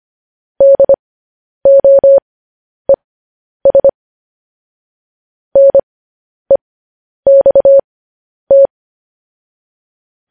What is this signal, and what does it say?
Morse code, heard as a keyed tone, about 25 wpm, spaced as for 10 wpm, 550 Hz. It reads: DOES NEXT
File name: podcast / News headlines-1x-25x10